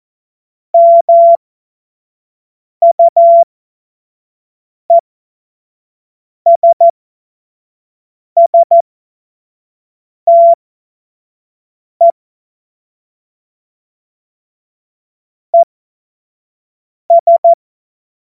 Morse code, heard as a keyed tone, about 14 wpm, spaced as for 5 wpm, 680 Hz. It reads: MUESSTE ES
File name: Morsekurs / Witze / Witz 08. November 2022 25 F70